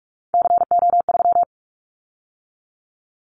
Morse code, heard as a keyed tone, about 44 wpm, 700 Hz. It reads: CO3
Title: Callsign Prefixes for Top 100 DXCC - 45wpm